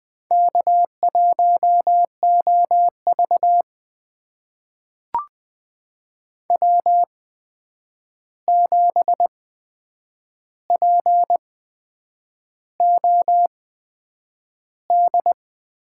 Morse code, heard as a keyed tone, about 20 wpm, spaced as for 8 wpm, 700 Hz. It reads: K1OV W7POD
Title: USA Call Signs - 8x Character Spacing - 20wpm